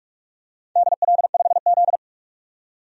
Morse code, heard as a keyed tone, about 44 wpm, 690 Hz. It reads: DL5Z
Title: DXCC-100 Call Signs - 4 Characters - 45wpm